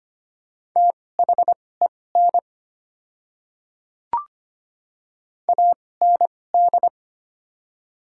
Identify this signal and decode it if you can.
Morse code, heard as a keyed tone, about 25 wpm, spaced as for 18 wpm, 690 Hz. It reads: THEN AND
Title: 2x Character Spacing - Top 100 Words - 25wpm